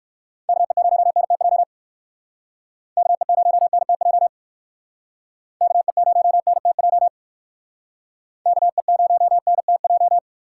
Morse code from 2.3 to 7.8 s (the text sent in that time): KE0NTJ KE0NTJ